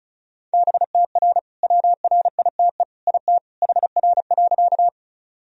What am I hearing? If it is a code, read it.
Morse code, heard as a keyed tone, about 36 wpm, 700 Hz. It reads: BTR WRITE IT HR.
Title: Sentences from Top 400 Words and Abbreviations 35wpm